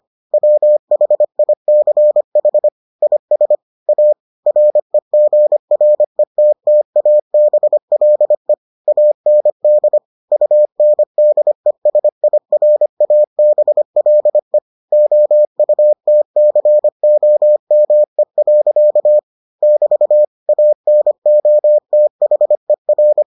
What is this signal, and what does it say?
Morse code, heard as a keyed tone, about 25 wpm, 600 Hz. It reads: WHICH IS A REGRETTABLE AND UNDESIRABLE OUTCOME. <BT> ANOTHER